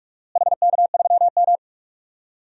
Morse code, heard as a keyed tone, about 48 wpm, 690 Hz. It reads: HK3M
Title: DXCC-100 Call Signs - 4 Characters - Repeated 3x - 45wpm